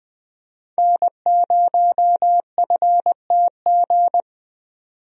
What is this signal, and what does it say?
Morse code, heard as a keyed tone, about 20 wpm, 700 Hz. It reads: N0FTG
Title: USA Call Signs - 5 Characters - Repeated 2x - 20wpm